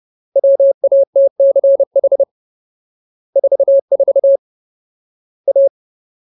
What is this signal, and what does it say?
Morse code, heard as a keyed tone, about 30 wpm, 550 Hz. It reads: WATCH 44 A
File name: Ultimate Word List / 30WPM Ultimate